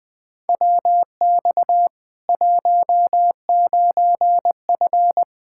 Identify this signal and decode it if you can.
Morse code, heard as a keyed tone, about 20 wpm, 700 Hz. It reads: WX 19F
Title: K1USN SST QSO Practice 20wpm